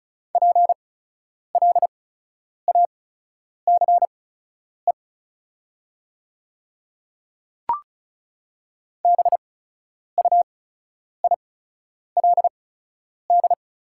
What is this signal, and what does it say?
Morse code, heard as a keyed tone, about 36 wpm, spaced as for 10 wpm, 700 Hz. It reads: PLACE BUILD
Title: 8x Character Spacing - Top 300 Words - 35wpm